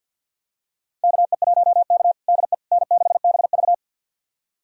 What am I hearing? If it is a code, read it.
Morse code, heard as a keyed tone, about 52 wpm, 690 Hz. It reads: KE1K DE N6BV
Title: USA Callsign DE Callsign - 4 Characters - 50wpm